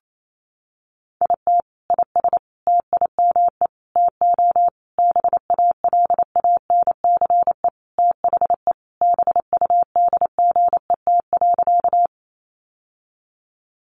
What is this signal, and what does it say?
Morse code, heard as a keyed tone, about 28 wpm, 700 Hz. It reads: IT IS TIME TO BALANCE THE BUDGET.
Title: Sentences from Top 1200 Words 28wpm